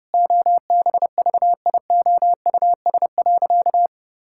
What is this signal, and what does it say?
Morse code, heard as a keyed tone, about 30 wpm, 700 Hz. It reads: OBVIOUS.
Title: Sentences from Top 1600 Words 30wpm